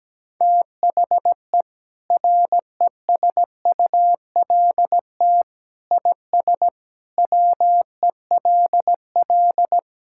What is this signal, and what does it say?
Morse code, heard as a keyed tone, about 17 wpm, 700 Hz. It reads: THE RESULT IS WELL
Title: Sentences from Top 1000 Words - Review Part 5 - 17wpm